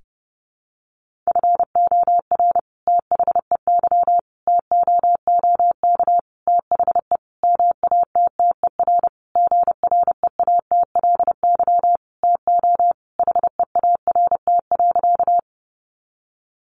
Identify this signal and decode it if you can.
Morse code, heard as a keyed tone, about 30 wpm, 700 Hz. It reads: FOR THEY TOOK THE MATTER GREATLY TO HEART.
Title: Sentences from Top 1400 Words 30wpm